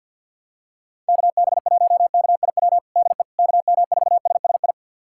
Morse code, heard as a keyed tone, about 53 wpm, 700 Hz. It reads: KB1KIW DE KM4SSI
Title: USA Callsign DE Callsign 50wpm